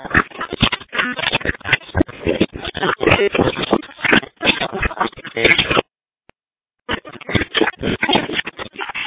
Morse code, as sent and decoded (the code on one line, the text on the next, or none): none